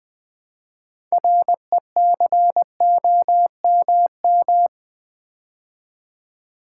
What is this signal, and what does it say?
Morse code, heard as a keyed tone, about 20 wpm, 700 Hz. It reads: RECOMM
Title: Build A Word - 3 Letters At A Time - 9-Letter Words - 20wpm